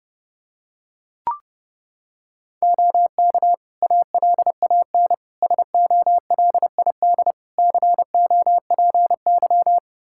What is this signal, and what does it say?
Morse code, heard as a keyed tone, about 30 wpm, 700 Hz. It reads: OK ALAN SOLID COPY
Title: Practice QSO - Exchange 3 - More Info - 30wpm